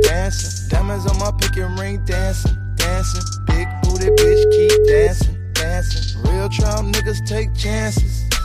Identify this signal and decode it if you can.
Ringback tone (Europe, 425 Hz).